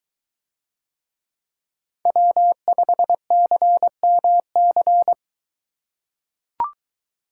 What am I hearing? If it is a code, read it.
Morse code, heard as a keyed tone, about 23 wpm, 700 Hz. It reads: W5CMC